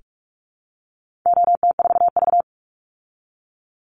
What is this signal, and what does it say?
Morse code, heard as a keyed tone, about 44 wpm, 700 Hz. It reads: OT4V
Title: Call Signs Repeated 2x - 45wpm